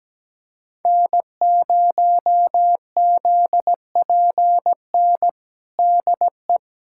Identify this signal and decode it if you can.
Morse code, heard as a keyed tone, about 17 wpm, 700 Hz. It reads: N0ZPN DE
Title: USA Callsign DE Callsign 17wpm